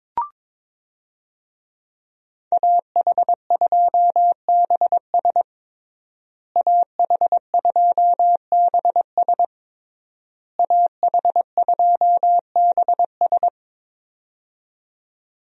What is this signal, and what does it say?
Morse code, heard as a keyed tone, about 22 wpm, 700 Hz. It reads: AH2BS AH2BS AH2BS
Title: USA Call Signs - 5 Characters - Repeated 3x - 22wpm